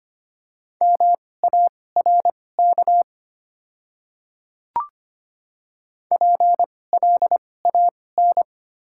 Morse code, heard as a keyed tone, about 25 wpm, spaced as for 18 wpm, 700 Hz. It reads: MARK PLAN
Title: 2x Character Spacing - Top 500 Words - 25wpm